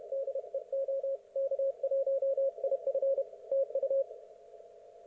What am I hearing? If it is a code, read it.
Morse code, heard as a keyed tone, about 31 wpm, 530 Hz. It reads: DE OK1IF TU
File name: Archiv / CUC107 / ok1fhi cul107